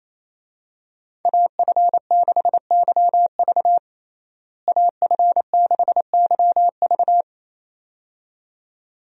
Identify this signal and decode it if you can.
Morse code, heard as a keyed tone, about 28 wpm, 700 Hz. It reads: AF6YV AF6YV
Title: USA Call Signs - 5 Characters - Repeated 2x - 28wpm